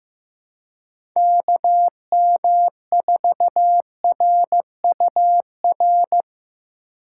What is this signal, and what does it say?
Morse code, heard as a keyed tone, about 16 wpm, 700 Hz. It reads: KM4RUR
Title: USA Call Signs - 6 Characters - 3x Character Spacing - 15wpm